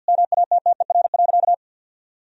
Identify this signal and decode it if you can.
Morse code, heard as a keyed tone, about 50 wpm, 700 Hz. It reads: MATTER.
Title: Sentences from Top 1000 Words - Review Part 3 - 50wpm